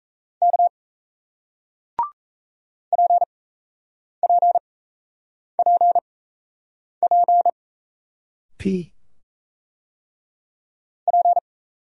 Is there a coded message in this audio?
Yes: Morse code from 0.4 to 11.4 s, heard as a keyed tone, about 38 wpm, spaced as for 8 wpm, 700 Hz. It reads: K PPPP P